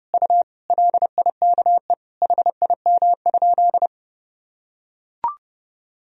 Morse code, heard as a keyed tone, about 30 wpm, 700 Hz. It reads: U LIKE HIM?
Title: Sentences from Top 100 Words and Abbreviations 30wpm